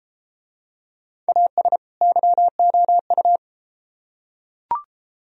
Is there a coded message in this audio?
Yes: Morse code from 1.3 to 3.4 s, heard as a keyed tone, about 30 wpm, 700 Hz. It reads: AS YOU